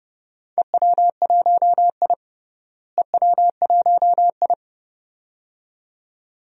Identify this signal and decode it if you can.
Morse code, heard as a keyed tone, about 32 wpm, 700 Hz. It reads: EW1I EW1I
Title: DXCC-100 Call Signs - 4 Characters - Repeated 2x - 30wpm